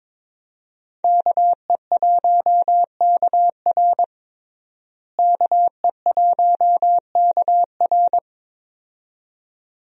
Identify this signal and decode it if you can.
Morse code, heard as a keyed tone, about 22 wpm, 700 Hz. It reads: KE1KR KE1KR
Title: USA Call Signs - 5 Characters - Repeated 2x - 22wpm